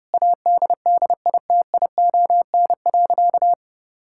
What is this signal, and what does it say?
Morse code, heard as a keyed tone, about 30 wpm, 700 Hz. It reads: ADDITION.
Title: Sentences from Top 1000 Words - Review Part 6 - 30wpm